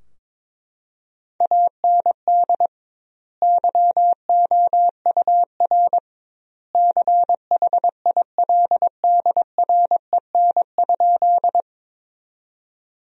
Morse code, heard as a keyed tone, about 22 wpm, 700 Hz. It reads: AND YOUR CHILDREN?
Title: 2x Word-Spacing - Sentences from Top 300 Words - 22wpm